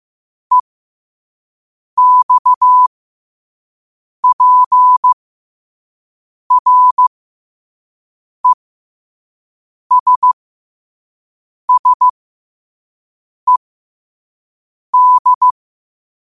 Morse code, heard as a keyed tone, about 15 wpm, spaced as for 5 wpm, 1000 Hz. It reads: EXPRESSED
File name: qotdfiles / 5wpm